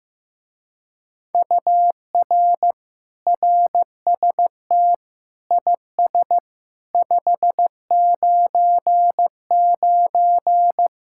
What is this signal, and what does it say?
Morse code, heard as a keyed tone, about 15 wpm, 700 Hz. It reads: UR RST IS 599
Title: Speed-Racing Common QSO Phrases Encore - 15wpm